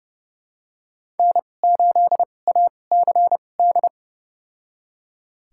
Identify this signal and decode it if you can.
Morse code, heard as a keyed tone, about 30 wpm, spaced as for 22 wpm, 700 Hz. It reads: N8ACD
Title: USA Call Signs - 5 Characters - 2x Character Spacing - 30wpm